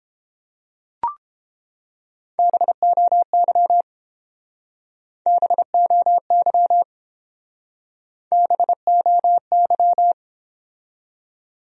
Morse code, heard as a keyed tone, about 30 wpm, 700 Hz. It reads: BOY BOY BOY